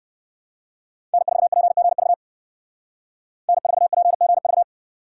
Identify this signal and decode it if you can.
Morse code, heard as a keyed tone, about 53 wpm, 700 Hz. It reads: N4PGV N4PGV
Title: Speed Racing - USA 1x3 Call Signs 45wpm